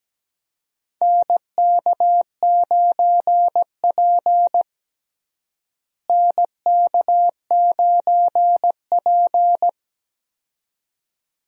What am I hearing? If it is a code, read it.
Morse code, heard as a keyed tone, about 17 wpm, 700 Hz. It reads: NK9P NK9P
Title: USA Call Signs - 4 Characters - Repeated 2x - 17wpm